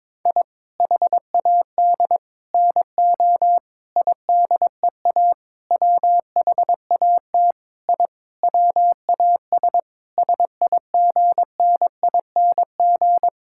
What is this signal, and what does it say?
Morse code, heard as a keyed tone, about 21 wpm, 700 Hz. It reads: I HAD NO IDEA WHAT I WAS SIGNING
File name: Sentences with 7-Word Phrases - 22wpm